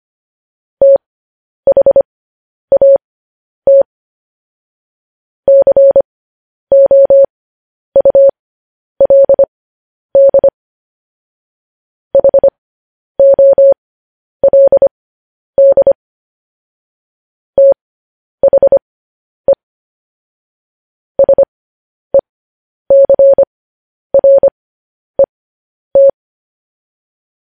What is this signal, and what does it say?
Morse code, heard as a keyed tone, about 25 wpm, spaced as for 10 wpm, 550 Hz. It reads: THAT COULD HOLD THE SECRET
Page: news headlines in morse code at 10 wpm